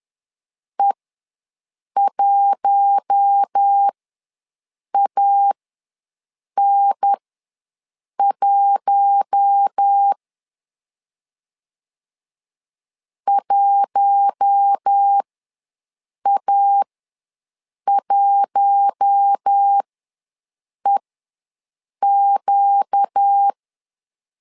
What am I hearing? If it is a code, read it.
Morse code, heard as a keyed tone, about 10 wpm, spaced as for 6 wpm, 800 Hz. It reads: E1AN1 1A1EQ
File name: LEZIONI / 4 Q1 ws 10wpm 2538